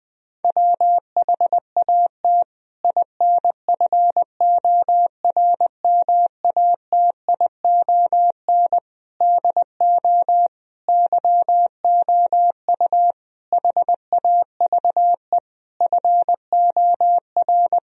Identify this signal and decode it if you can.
Morse code, heard as a keyed tone, about 20 wpm, 700 Hz. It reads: WHAT INFORMATION DO YOU HAVE FOR
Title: Sentences from Top 1000 Words - Review Part 3 - 20wpm